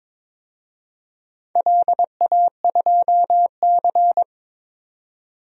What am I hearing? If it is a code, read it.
Morse code, heard as a keyed tone, about 22 wpm, 700 Hz. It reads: LA2C
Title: DXCC-100 Call Signs - 4 Characters - 22wpm